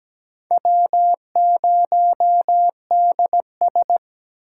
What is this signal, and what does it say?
Morse code, heard as a keyed tone, about 17 wpm, 700 Hz. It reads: W0DS
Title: USA Call Signs - 2x Character Spacing - 17wpm